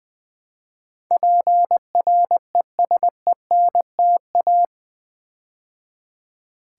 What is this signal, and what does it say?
Morse code, heard as a keyed tone, about 20 wpm, 700 Hz. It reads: PRESENTA
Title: Build A Word - 4 Letters At A Time - 12-Letter Words - 20wpm